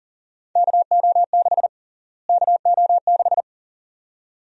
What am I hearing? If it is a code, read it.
Morse code, heard as a keyed tone, about 40 wpm, 700 Hz. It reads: KO6 KO6
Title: USA Call Sign Prefixes - Repeated 2x - 40wpm